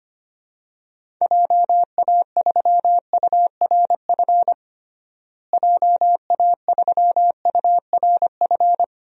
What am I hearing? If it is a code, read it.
Morse code, heard as a keyed tone, about 25 wpm, 700 Hz. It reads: JA3URF JA3URF
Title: DXCC-100 Call Signs - 6 Characters - Repeated 2x - 25wpm